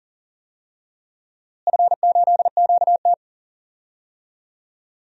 Morse code, heard as a keyed tone, about 40 wpm, 700 Hz. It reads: F8QT
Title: DXCC-100 Call Signs - 4 Characters - Repeated 3x - 40wpm